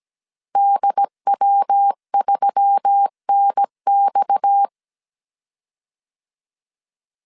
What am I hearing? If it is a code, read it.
Morse code, heard as a keyed tone, about 17 wpm, 800 Hz. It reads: DW3NX
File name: CW / DET / 1 6 1500C S A 16wpm 2635